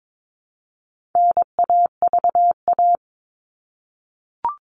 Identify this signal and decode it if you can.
Morse code, heard as a keyed tone, about 22 wpm, 700 Hz. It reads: NAVA
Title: All Cut Numbers - 4-Digit 22wpm